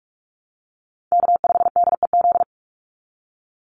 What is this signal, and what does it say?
Morse code, heard as a keyed tone, about 44 wpm, 690 Hz. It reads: K5DEZ
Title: NA SOTA Chaser Callsigns 45wpm